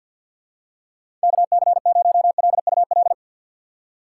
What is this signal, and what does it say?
Morse code, heard as a keyed tone, about 51 wpm, 700 Hz. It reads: KC0LUL
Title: USA Call Signs - 6 Characters - 6x Character Spacing - 50wpm